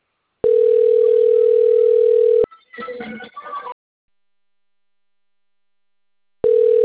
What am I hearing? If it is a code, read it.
Ringback tone (North America), 440 + 480 Hz.